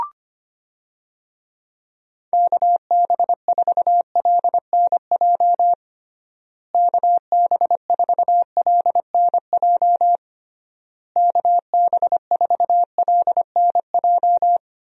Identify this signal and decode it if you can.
Morse code, heard as a keyed tone, about 25 wpm, 700 Hz. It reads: KB4LNJ KB4LNJ KB4LNJ
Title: USA Call Signs - 6 Characters - Repeated 3x - 25wpm